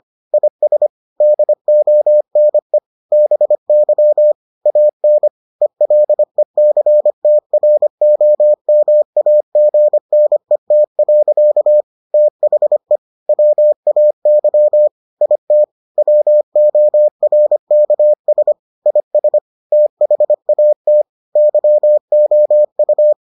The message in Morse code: IS DONE BY AN ELECTROMAGNET. THE WAY IT WORKS IS THAT YOU

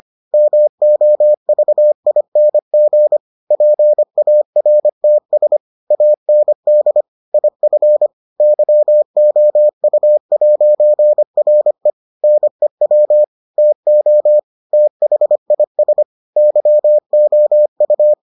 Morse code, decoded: MOVING PARTS AND IF YOU'RE NEW TO THIS YOU

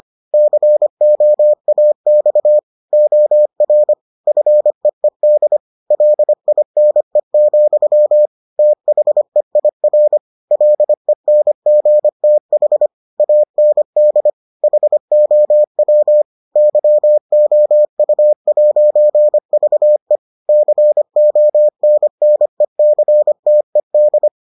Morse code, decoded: COAX OR FEED LINE, THEIR LENGTH AND HOW YOU'VE CONNECTED

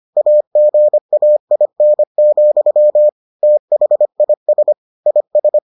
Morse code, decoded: AGAIN, THIS IS

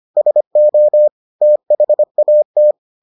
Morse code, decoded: SO THAT

